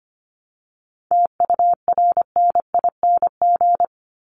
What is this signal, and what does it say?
Morse code, heard as a keyed tone, about 25 wpm, 700 Hz. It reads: TURNING
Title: ING Suffix - Top 100 Words with Base Words 25wpm